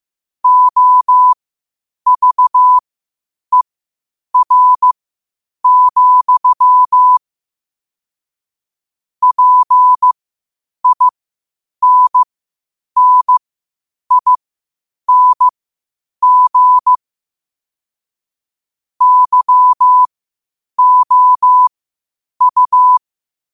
Morse code, heard as a keyed tone, about 15 wpm, spaced as for 8 wpm, 1000 Hz. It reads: OVER, PINNING YOU
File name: qotdfiles / 8wpm